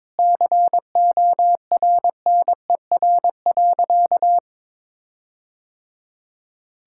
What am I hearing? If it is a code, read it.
Morse code, heard as a keyed tone, about 22 wpm, 700 Hz. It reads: CORNER.
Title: Sentences from Top 1600 Words 22wpm